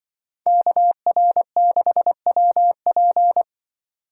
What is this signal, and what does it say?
Morse code, heard as a keyed tone, about 25 wpm, 700 Hz. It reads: KR6WP
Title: Speed Racing - USA 2x2 Call Signs 20wpm